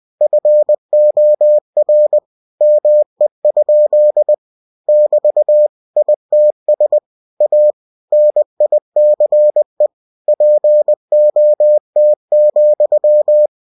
Morse code, heard as a keyed tone, about 20 wpm, 600 Hz. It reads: FOR ME? <BT> ITS A NICE POT,
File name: cw-books / winnie-the-pooh / winnie-20wpm-page-0051